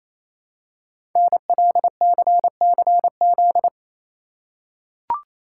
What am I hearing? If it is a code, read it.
Morse code, heard as a keyed tone, about 27 wpm, 700 Hz. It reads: NLCCZ
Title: Code Groups - 5 Letters - 28wpm